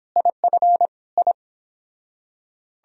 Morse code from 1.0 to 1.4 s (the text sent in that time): I